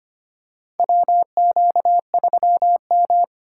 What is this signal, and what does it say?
Morse code, heard as a keyed tone, about 25 wpm, 700 Hz. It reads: WQ3M
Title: USA Call Signs - 4 Characters - 4x Character Spacing - 25wpm